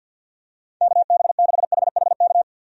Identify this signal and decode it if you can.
Morse code, heard as a keyed tone, about 51 wpm, 700 Hz. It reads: KB6HHK
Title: USA Call Signs - 6 Characters - 50wpm